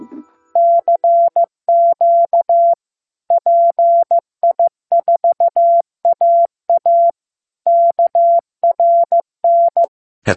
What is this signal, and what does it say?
Morse code, heard as a keyed tone, about 15 wpm, 680 Hz. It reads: CQ PI4AA KRN
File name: pi4aa / PI4AA Uitzending